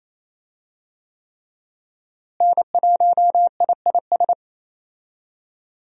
Morse code, heard as a keyed tone, about 27 wpm, 700 Hz. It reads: N1IIS
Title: USA Call Signs - 5 Characters - Repeated 3x - 28wpm